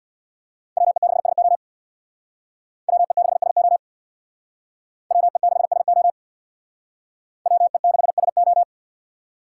Morse code, heard as a keyed tone, about 64 wpm, 700 Hz. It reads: WE6SO WE6SO WE6SO WE6SO